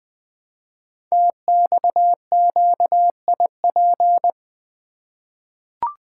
Morse code, heard as a keyed tone, about 20 wpm, 700 Hz. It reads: TXQIP
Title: Code Groups - 5 Letters - 20wpm